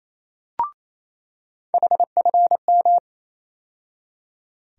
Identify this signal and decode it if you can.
Morse code, heard as a keyed tone, about 28 wpm, 700 Hz. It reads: HFM
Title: USA Call Sign Postfixes - 3 Letters 28wpm